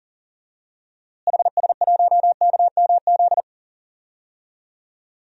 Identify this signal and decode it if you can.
Morse code, heard as a keyed tone, about 38 wpm, 700 Hz. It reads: HS1KMZ
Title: DXCC-100 Call Signs - 6 Characters - Repeated 2x - 40wpm